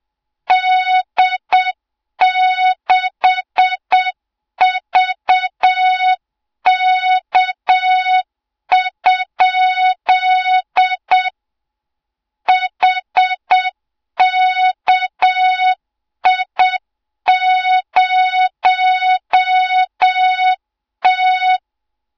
Morse code, heard as a keyed tone, about 8 wpm, 3700 Hz, with noise. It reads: D6VK? HKI0T
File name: Morze-HA2MN / hang / 20 GYAKORLAT